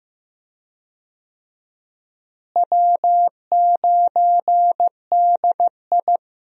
Morse code, heard as a keyed tone, about 15 wpm, 700 Hz. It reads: W9DI
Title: USA Call Signs - 4 Characters - Repeated 3x - 15wpm